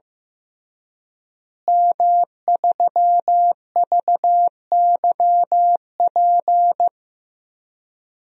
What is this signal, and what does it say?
Morse code, heard as a keyed tone, about 15 wpm, 700 Hz. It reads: M3VYP
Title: DXCC-100 Call Signs - 5 Characters - Repeated 3x - 15wpm